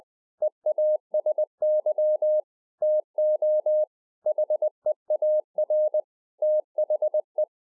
Morse code, heard as a keyed tone, about 20 wpm, 610 Hz. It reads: EASY TO HEAR THE
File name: 2011-04-mp3 / A SMALL TRANCEIVER 3 20WPM